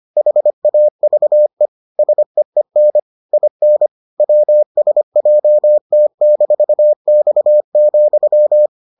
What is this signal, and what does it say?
Morse code, heard as a keyed tone, about 25 wpm, 600 Hz. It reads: HAVE SEEN IN WSJT-X,